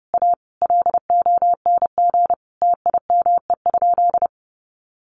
Morse code, heard as a keyed tone, about 30 wpm, 700 Hz. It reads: A LONG TIME?
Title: Sentences from Top 1300 Words 30wpm